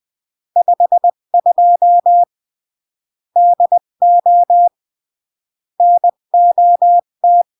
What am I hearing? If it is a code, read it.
Morse code, heard as a keyed tone, about 20 wpm, 700 Hz. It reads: 52 DO NOT
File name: CW / jodorowsky / 82 commandments 20-15